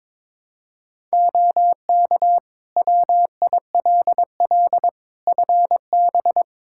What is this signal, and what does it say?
Morse code, heard as a keyed tone, about 22 wpm, 700 Hz. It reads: OK WILL FB
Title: Practice QSO - Exchange 2 - Info - 22wpm